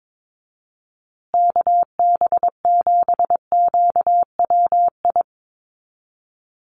Morse code, heard as a keyed tone, about 22 wpm, 700 Hz. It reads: KB7QWI
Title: USA Call Signs Repeated 3x 22wpm